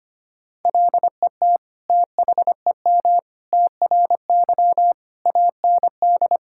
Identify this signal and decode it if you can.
Morse code, heard as a keyed tone, about 25 wpm, 700 Hz. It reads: LET THEM TRY AND
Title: Sentences from Top 1000 Words - Review Part 5 - 25wpm